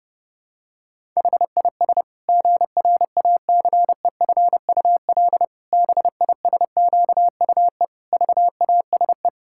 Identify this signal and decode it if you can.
Morse code, heard as a keyed tone, about 30 wpm, 700 Hz. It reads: HIS GRACEFUL BISQUE VASE
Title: Pangram Challenge v2 - 30wpm